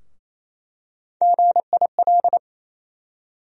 Morse code, heard as a keyed tone, about 28 wpm, 700 Hz. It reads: GIL
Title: USA Call Sign Postfixes - 3 Letters 28wpm